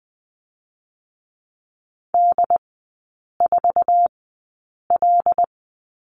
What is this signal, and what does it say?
Morse code, heard as a keyed tone, about 20 wpm, spaced as for 8 wpm, 700 Hz. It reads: D4L
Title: 100 Lesson 17e - Letter u - Three Characters 20wpm